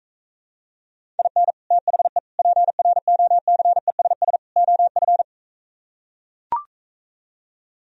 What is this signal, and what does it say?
Morse code, heard as a keyed tone, about 44 wpm, 700 Hz. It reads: IN THE PROCESS OF